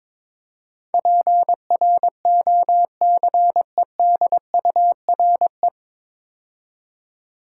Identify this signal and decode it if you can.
Morse code, heard as a keyed tone, about 22 wpm, 700 Hz. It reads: PROCEDURE
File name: 9 Letter Words - First 3 Letters Given - 22wpm